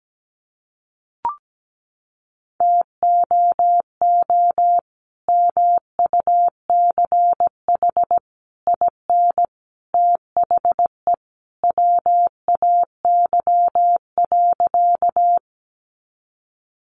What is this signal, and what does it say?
Morse code, heard as a keyed tone, about 17 wpm, 700 Hz. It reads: TOO MUCH IN THE WAY.
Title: Sentences from Top 200 Words 17wpm